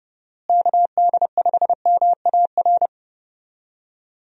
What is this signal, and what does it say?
Morse code, heard as a keyed tone, about 32 wpm, 700 Hz. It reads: KD5MAR
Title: USA Call Signs - 6 Characters - Repeated 3x - 30wpm